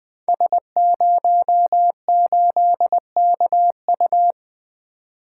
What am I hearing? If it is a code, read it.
Morse code, heard as a keyed tone, about 20 wpm, 700 Hz. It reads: S08KU
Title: Code Groups - 5 Letter:Numbers - 20wpm